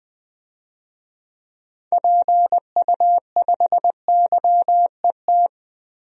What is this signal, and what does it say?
Morse code, heard as a keyed tone, about 20 wpm, 700 Hz. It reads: PU5YET
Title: DXCC-100 Call Signs - 6 Characters - 20wpm